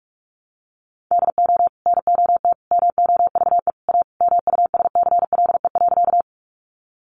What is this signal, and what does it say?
Morse code, heard as a keyed tone, about 44 wpm, 700 Hz. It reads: DO NOT MOVE A MUSCLE.
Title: Sentences from Top 1300 Words 45wpm